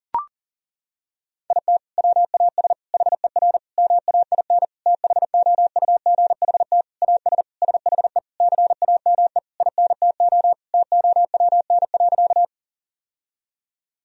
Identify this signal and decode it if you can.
Morse code, heard as a keyed tone, about 37 wpm, 700 Hz. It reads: IT WAS HER MAIN THOUGHT AS SHE CAME INTO TOWN.
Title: Sentences from Top 1000 Words - Review Part 2 - 40wpm